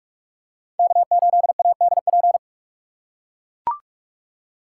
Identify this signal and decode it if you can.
Morse code, heard as a keyed tone, about 44 wpm, 700 Hz. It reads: K8ADP